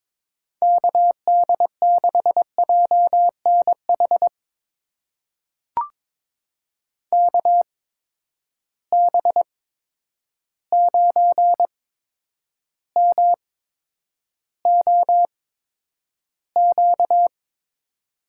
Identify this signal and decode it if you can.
Morse code, heard as a keyed tone, about 22 wpm, spaced as for 9 wpm, 700 Hz. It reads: KD6JNH KB9MOQ
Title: USA Call Signs - 8x Character Spacing - 22wpm